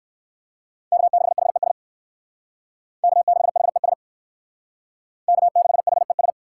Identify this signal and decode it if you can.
Morse code, heard as a keyed tone, about 60 wpm, 700 Hz. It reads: K6HES K6HES K6HES